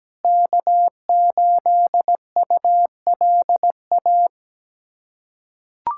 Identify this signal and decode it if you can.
Morse code, heard as a keyed tone, about 17 wpm, 700 Hz. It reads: K8ULA
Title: USA Callsign DE Callsign 17wpm